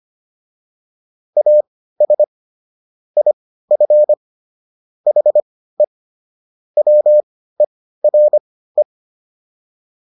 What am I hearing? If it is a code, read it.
Morse code, heard as a keyed tone, about 25 wpm, spaced as for 15 wpm, 600 Hz. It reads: AS IF HE WERE